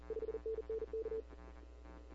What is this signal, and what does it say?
Morse code, heard as a keyed tone, about 40 wpm, 440 Hz. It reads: 5NN K
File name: CW / 2005-k5tr arrl cw dx -15:00